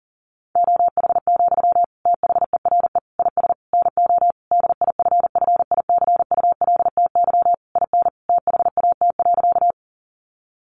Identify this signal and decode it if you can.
Morse code, heard as a keyed tone, about 40 wpm, 700 Hz. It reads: OH, THERE IS NO DIFFICULTY IN THAT.
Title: Sentences from Top 1500 Words 40wpm